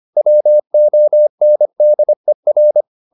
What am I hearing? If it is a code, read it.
Morse code, heard as a keyed tone, about 25 wpm, 600 Hz. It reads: WONDER
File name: the old man and the sea part 1 25wpm